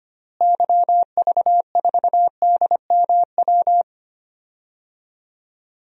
Morse code, heard as a keyed tone, about 25 wpm, 700 Hz. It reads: YV4DMW
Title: DXCC-100 Call Signs - 6 Characters - Repeated 2x - 25wpm